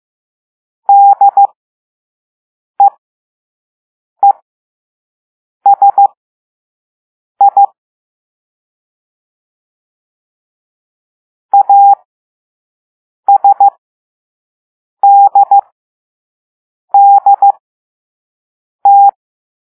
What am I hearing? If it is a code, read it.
Morse code, heard as a keyed tone, about 15 wpm, spaced as for 5 wpm, 800 Hz. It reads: DEESI ASDDT